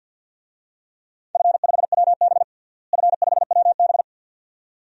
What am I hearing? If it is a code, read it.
Morse code, heard as a keyed tone, about 52 wpm, 700 Hz. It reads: F5WB F5WB
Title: DXCC-100 Call Signs - 4 Characters - Repeated 2x - 50wpm